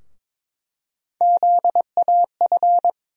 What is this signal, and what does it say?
Morse code, heard as a keyed tone, about 22 wpm, 700 Hz. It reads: ZAF
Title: USA Call Sign Postfixes - 3 Letters 22wpm